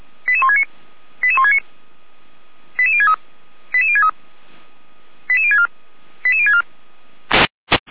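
Five-tone selective call (ZVEI), heard as quick runs of five tones, tones of about 70 ms. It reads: 88169 88169 88063 88063 88064 88064
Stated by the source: ZVEI five-tone selective call received by SDR.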